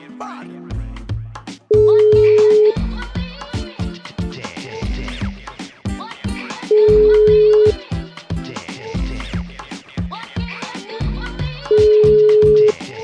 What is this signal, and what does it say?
Ringback tone (Europe, 425 Hz), with speech.